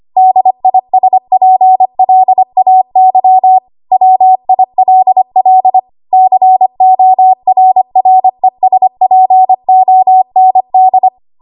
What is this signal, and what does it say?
Morse code, heard as a keyed tone, about 25 wpm, 750 Hz. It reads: DISPLAY WILL CORRESPOND
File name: Archive / 25 WPM / 25WPM